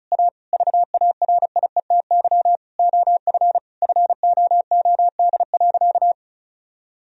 Morse code, heard as a keyed tone, about 34 wpm, 700 Hz. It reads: A VARIETY OF FOOD.
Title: Sentences from Top 1000 Words - Review Part 4 - 35wpm